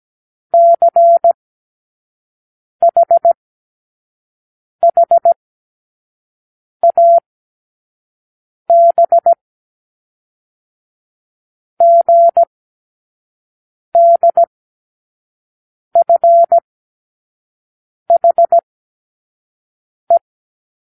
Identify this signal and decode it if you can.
Morse code, heard as a keyed tone, about 17 wpm, spaced as for 5 wpm, 680 Hz. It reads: CHHAB GDFHE
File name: k7qo / Lesson12(A-H)